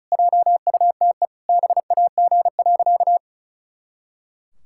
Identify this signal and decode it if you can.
Morse code, heard as a keyed tone, about 36 wpm, 700 Hz. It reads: JUTE BAG.
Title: Pangram Challenge v2 - 35wpm